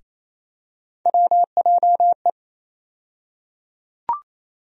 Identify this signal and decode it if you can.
Morse code, heard as a keyed tone, about 28 wpm, 700 Hz. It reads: WJE
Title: USA Call Sign Postfixes - 3 Letters 28wpm